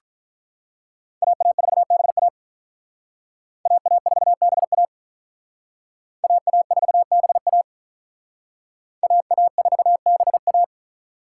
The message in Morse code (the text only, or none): AA4BA AA4BA AA4BA AA4BA